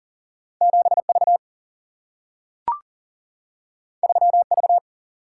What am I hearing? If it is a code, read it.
Morse code, heard as a keyed tone, about 40 wpm, 700 Hz. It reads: 7 V 3 V
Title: Single-Letter Single-Number 40wpm